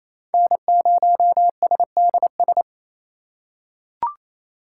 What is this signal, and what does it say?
Morse code, heard as a keyed tone, about 28 wpm, 700 Hz. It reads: N0SDS
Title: USA Call Signs - 3x Character Spacing - 28wpm